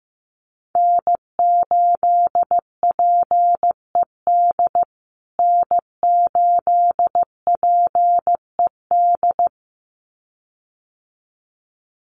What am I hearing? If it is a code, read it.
Morse code, heard as a keyed tone, about 15 wpm, 700 Hz. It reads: N8PED N8PED
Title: USA Call Signs Repeated 2x 15wpm